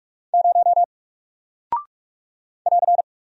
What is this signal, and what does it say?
Morse code, heard as a keyed tone, about 44 wpm, 700 Hz. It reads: 0 <AR>
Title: Single Character (Letter:Number:Punctuation:Prosign) Alt 45wpm